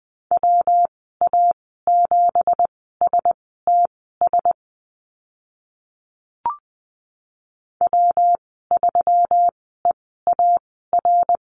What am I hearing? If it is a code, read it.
Morse code, heard as a keyed tone, about 20 wpm, spaced as for 15 wpm, 700 Hz. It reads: WA7STS W3EAR
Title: 173 Lesson 28g - Number 9 - Call Signs 20wpm